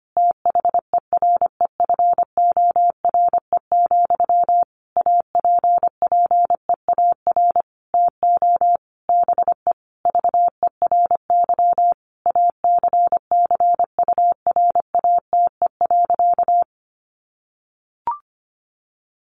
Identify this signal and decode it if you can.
Morse code, heard as a keyed tone, about 25 wpm, 700 Hz. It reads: THEREFORE, APPEAR TO BE VERY ACCURATE.
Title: Sentences from Top 1200 Words 25wpm